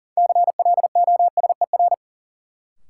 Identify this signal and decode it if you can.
Morse code, heard as a keyed tone, about 38 wpm, 700 Hz. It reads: CLOSER
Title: Sets of 5 Words Encore - New Word Drill - 40wpm